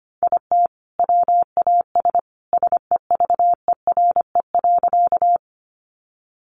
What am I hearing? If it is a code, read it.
Morse code, heard as a keyed tone, about 25 wpm, 700 Hz. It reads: IT WAS SEVERE.
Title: Sentences from Top 1400 Words 25wpm